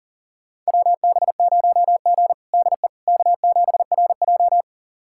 Morse code, heard as a keyed tone, about 40 wpm, 700 Hz. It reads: WB0G DE K7RJ
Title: USA Callsign DE Callsign - 4 Characters - 40wpm